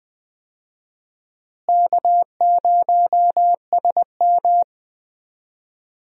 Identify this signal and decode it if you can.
Morse code, heard as a keyed tone, about 20 wpm, 700 Hz. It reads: K0SM
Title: USA Call Signs - 4 Characters - Repeated 2x - 20wpm